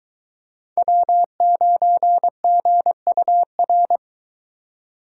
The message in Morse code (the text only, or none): W9GUR